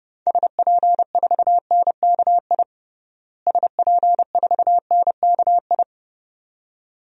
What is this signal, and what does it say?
Morse code, heard as a keyed tone, about 30 wpm, 700 Hz. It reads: SP4NKI SP4NKI
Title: DXCC-100 Call Signs - 6 Characters - Repeated 2x - 30wpm